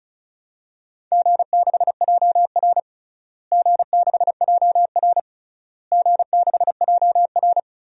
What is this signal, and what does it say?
Morse code, heard as a keyed tone, about 34 wpm, 700 Hz. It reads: G6JR G6JR G6JR
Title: DXCC-100 Call Signs - 4 Characters - Repeated 3x - 35wpm